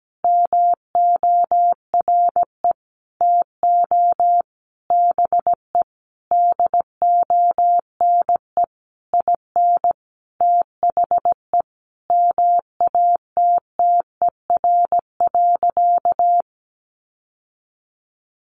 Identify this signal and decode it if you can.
Morse code, heard as a keyed tone, about 17 wpm, 700 Hz. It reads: MORE TO BE DONE IN THE MATTER.
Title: Sentences from Top 700 Words 17wpm